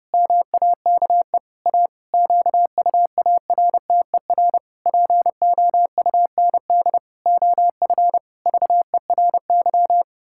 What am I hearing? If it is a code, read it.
Morse code, heard as a keyed tone, about 30 wpm, 700 Hz. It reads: MAKE A QUARTER POUND OF VERY
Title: Pangram Challenge v2 - 30wpm